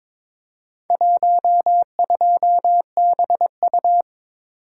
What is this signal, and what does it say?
Morse code, heard as a keyed tone, about 21 wpm, 700 Hz. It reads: 12BU
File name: practice / 22wpm / Four Letter-Number Combinations - 22wpm